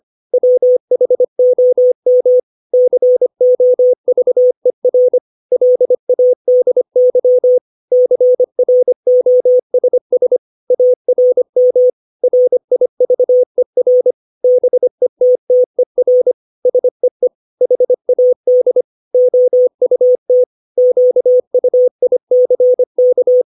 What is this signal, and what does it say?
Morse code, heard as a keyed tone, about 25 wpm, 500 Hz. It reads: WHOM COVER LADY CROSS ARM RIVER BETTER SEE HAD OUT QUICK
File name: 632 words / 632 Words 25WPM